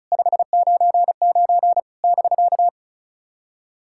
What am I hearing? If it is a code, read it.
Morse code, heard as a keyed tone, about 36 wpm, 700 Hz. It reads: 599 <BK>
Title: SOTA QSO Practice 35wpm